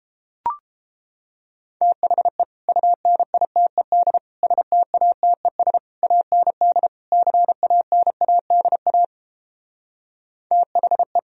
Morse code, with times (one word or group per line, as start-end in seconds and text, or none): 1.8-2.4 THE
2.7-4.2 UNITED
4.4-5.8 STATES
6.0-6.9 AND
7.1-9.0 CANADA
10.5-11.2 THE